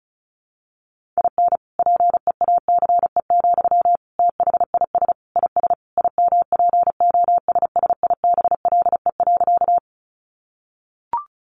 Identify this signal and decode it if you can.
Morse code, heard as a keyed tone, about 34 wpm, 700 Hz. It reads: IN PEACE, THIS IS IMPOSSIBLE.
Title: Sentences from Top 1500 Words 35wpm